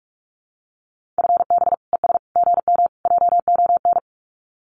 Morse code, heard as a keyed tone, about 46 wpm, 700 Hz. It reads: FB ES GM JON
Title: Common QSO Phrases Encore 45wpm